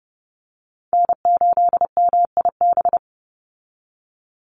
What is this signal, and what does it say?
Morse code, heard as a keyed tone, about 30 wpm, 700 Hz. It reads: N8MIB
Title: USA Call Signs 30wpm